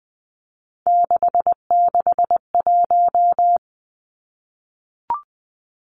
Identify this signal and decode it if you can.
Morse code, heard as a keyed tone, about 20 wpm, 700 Hz. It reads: 661
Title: Numbers - 3-Digit 20wpm